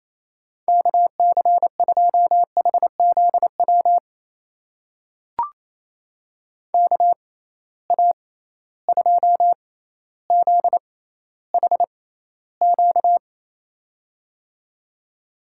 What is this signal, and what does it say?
Morse code, heard as a keyed tone, about 28 wpm, spaced as for 14 wpm, 700 Hz. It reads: KC2HZW KA2ZHQ
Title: USA Call Signs - 6 Characters - 6x Character Spacing - 28wpm